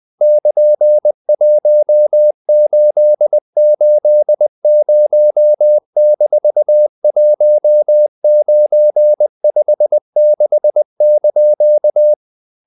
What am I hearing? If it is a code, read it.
Morse code, heard as a keyed tone, about 20 wpm, 600 Hz. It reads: <KN>1880-1956)
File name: CW / Practice / EnglishQuotations0320